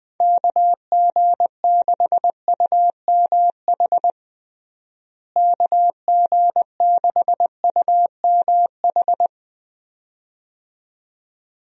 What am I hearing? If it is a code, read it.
Morse code, heard as a keyed tone, about 20 wpm, 700 Hz. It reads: KG6UMH KG6UMH
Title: USA Call Signs - 6 Characters - Repeated 2x - 20wpm